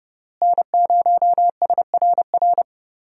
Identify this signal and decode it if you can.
Morse code, heard as a keyed tone, about 30 wpm, 700 Hz. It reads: N0SRR
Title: USA Call Signs - 5 Characters - Repeated 2x - 30wpm